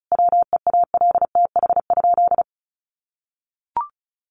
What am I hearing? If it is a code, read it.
Morse code, heard as a keyed tone, about 34 wpm, 700 Hz. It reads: WEALTH?
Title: Sentences from Top 1200 Words 35wpm